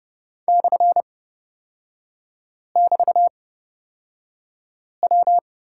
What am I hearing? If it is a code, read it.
Morse code, heard as a keyed tone, about 30 wpm, spaced as for 5 wpm, 700 Hz. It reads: /<BT>W
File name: Lesson 36e - BT - Three Characters 30wpm